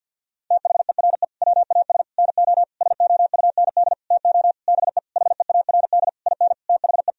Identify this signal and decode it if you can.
Morse code, heard as a keyed tone, about 51 wpm, 700 Hz. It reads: THERE WAS NO SOUND TO BE HEARD IN THE
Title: Sentences from Top 1000 Words - Review Part 3 - 50wpm